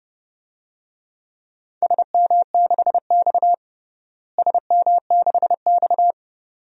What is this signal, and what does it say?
Morse code, heard as a keyed tone, about 32 wpm, 700 Hz. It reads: SM6X SM6X
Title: DXCC-100 Call Signs - 4 Characters - Repeated 2x - 30wpm